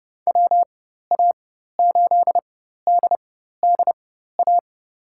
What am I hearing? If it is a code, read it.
Morse code, heard as a keyed tone, about 30 wpm, spaced as for 14 wpm, 700 Hz. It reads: WA8DDA
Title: USA Call Signs - 6 Characters - 4x Character Spacing - 30wpm